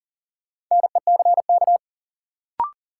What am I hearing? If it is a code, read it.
Morse code, heard as a keyed tone, about 40 wpm, 700 Hz. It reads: NECK
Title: Sets of 5 Words Encore - New Word Drill - 40wpm